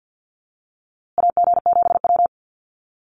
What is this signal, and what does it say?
Morse code, heard as a keyed tone, about 50 wpm, 700 Hz. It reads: AG7W
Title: USA Call Signs Repeated 3x 50wpm